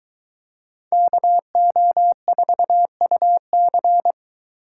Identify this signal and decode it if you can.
Morse code, heard as a keyed tone, about 22 wpm, 700 Hz. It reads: KO4UC